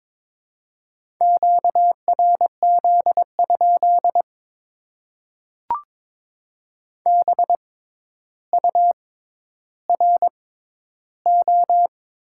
Morse code, heard as a keyed tone, about 22 wpm, spaced as for 11 wpm, 700 Hz. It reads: QRZ? BURO
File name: Top 90 QSO Elements - 6x Character Spacing - 22wpm